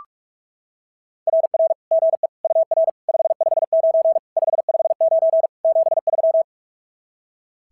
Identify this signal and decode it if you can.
Morse code, heard as a keyed tone, about 44 wpm, 690 Hz. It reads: RR GE UR 559 559 73
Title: SOTA QSO Practice 45wpm